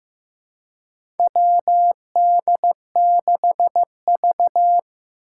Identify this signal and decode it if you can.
Morse code, heard as a keyed tone, about 15 wpm, 700 Hz. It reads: WD6V
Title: Four Challenging Letter-Number Combinations 15wpm